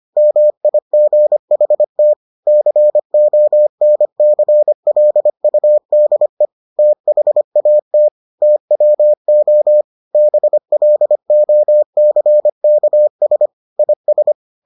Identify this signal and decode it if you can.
Morse code, heard as a keyed tone, about 25 wpm, 600 Hz. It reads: MIGHT CONCLUDE THAT TWO BLOCKS IS